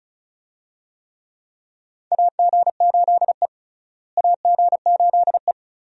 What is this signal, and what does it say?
Morse code, heard as a keyed tone, about 36 wpm, 700 Hz. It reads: AG8E AG8E
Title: USA Call Signs - 4 Characters - Repeated 2x - 35wpm